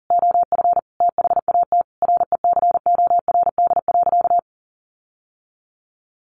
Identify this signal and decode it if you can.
Morse code, heard as a keyed tone, about 37 wpm, 700 Hz. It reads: OF THAT RECORD.
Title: Sentences from Top 1400 Words 40wpm